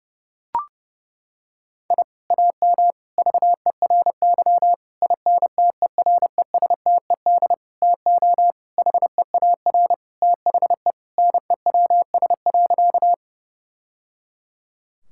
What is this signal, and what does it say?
Morse code, heard as a keyed tone, about 30 wpm, 700 Hz. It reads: I AM VERY INTERESTED TO HEAR THE NEWS.
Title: Sentences from Top 1000 Words - Review Part 6 - 30wpm